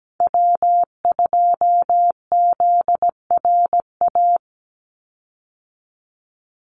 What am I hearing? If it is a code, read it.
Morse code, heard as a keyed tone, about 17 wpm, 700 Hz. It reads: W2ZRA
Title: Call Signs - 17wpm